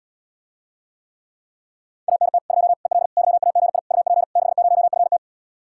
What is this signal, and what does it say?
Morse code, heard as a keyed tone, about 77 wpm, 700 Hz. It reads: ANT 2 EL YAGI UP 60FT